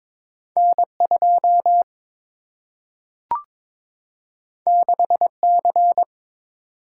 Morse code, heard as a keyed tone, about 22 wpm, 700 Hz. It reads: N2 6C
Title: Single-Letter Single-Number 22wpm